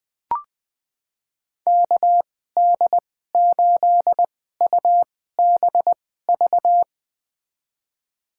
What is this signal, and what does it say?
Morse code, heard as a keyed tone, about 21 wpm, spaced as for 15 wpm, 700 Hz. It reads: KD8UBV
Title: USA Call Signs - 6 Characters - 2x Character Spacing - 20wpm